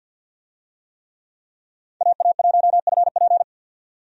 Morse code, heard as a keyed tone, about 51 wpm, 700 Hz. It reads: AA1FP
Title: USA Call Signs - 5 Characters - 50wpm